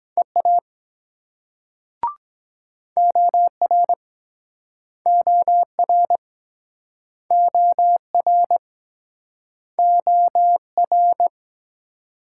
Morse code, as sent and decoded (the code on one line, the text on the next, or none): . .- / --- .-. / --- .-. / --- .-. / --- .-.
EA OR OR OR OR